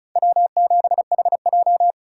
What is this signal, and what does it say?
Morse code, heard as a keyed tone, about 45 wpm, 700 Hz. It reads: W7HJ